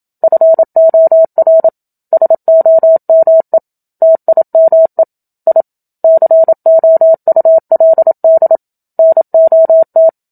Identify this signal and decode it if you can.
Morse code, heard as a keyed tone, about 27 wpm, 650 Hz. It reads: FOR SOME TIME I COULD NOT
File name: Book Two / 01-Chapter 1 - 27 WPM